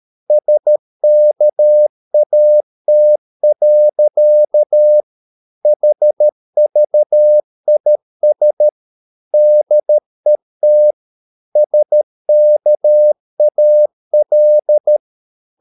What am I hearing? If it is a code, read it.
Morse code, heard as a keyed tone, about 13 wpm, 600 Hz. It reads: SKAT. HVIS DET SKAL
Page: Ferie 13 wpm